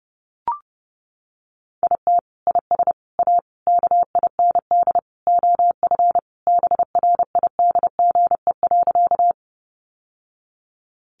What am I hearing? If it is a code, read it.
Morse code, heard as a keyed tone, about 30 wpm, 700 Hz. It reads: IT IS A KIND OF BRIDGE.
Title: Sentences from Top 1200 Words 30wpm